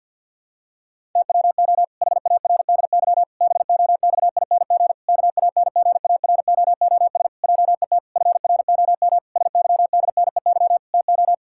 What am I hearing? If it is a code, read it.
Morse code, heard as a keyed tone, about 47 wpm, 700 Hz. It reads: TWO HARDY BOXING KANGAROOS JET FROM SYDNEY TO
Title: Pangram Challenge v2 - 50wpm